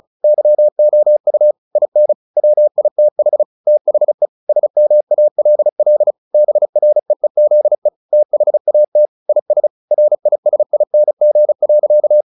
Morse code, heard as a keyed tone, about 34 wpm, 600 Hz. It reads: YOU IN WITH THE SMALL BREEZE THAT IS RISING.
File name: the old man and the sea / the old man and the sea 1 35wpm